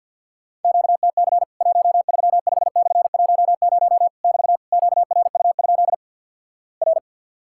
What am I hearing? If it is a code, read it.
Morse code, heard as a keyed tone, about 50 wpm, 700 Hz. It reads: QTC 135/10 <BT> QRU? R
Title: Worked All Europe DX Contest - QTC Exchange - 50wpm